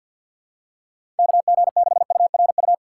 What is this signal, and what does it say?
Morse code, heard as a keyed tone, about 47 wpm, 700 Hz. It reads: KG6URU
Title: USA Call Signs - 4x Character Spacing - 50wpm